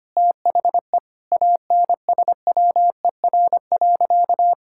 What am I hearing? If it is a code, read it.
Morse code, heard as a keyed tone, about 25 wpm, 700 Hz. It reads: THE ANSWER.
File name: practice / 25wpm / Sentences with 4-Word Phrases - 25wpm